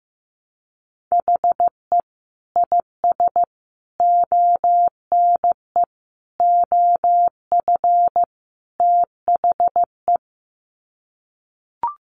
Morse code, heard as a keyed tone, about 15 wpm, 700 Hz. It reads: HE IS ONE OF THE
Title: Sets of 5 Words Encore 15wpm